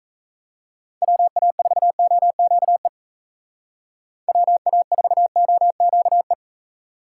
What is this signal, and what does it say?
Morse code, heard as a keyed tone, about 34 wpm, 700 Hz. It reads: WA4OQE WA4OQE
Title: Speed Racing - USA Call Signs 28wpm